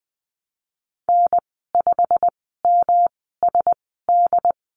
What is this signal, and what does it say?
Morse code, heard as a keyed tone, about 19 wpm, spaced as for 14 wpm, 700 Hz. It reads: N5MSD
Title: 128 Lesson 21g - Number 6 - Call Signs 20wpm